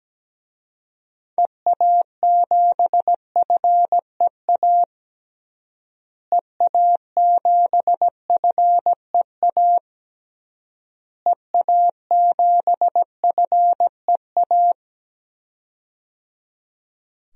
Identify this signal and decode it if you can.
Morse code, heard as a keyed tone, about 17 wpm, 700 Hz. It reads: EA7FEA EA7FEA EA7FEA
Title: DXCC-100 Call Signs - 6 Characters - Repeated 3x - 17wpm